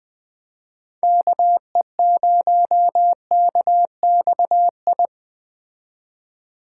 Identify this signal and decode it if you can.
Morse code, heard as a keyed tone, about 20 wpm, 700 Hz. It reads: KE0KXI
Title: Speed Racing - USA 2x3 Call Signs 17wpm